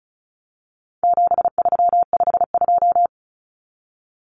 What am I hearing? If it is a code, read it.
Morse code, heard as a keyed tone, about 34 wpm, 700 Hz. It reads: 7352
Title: Numbers - 4-Digit 35wpm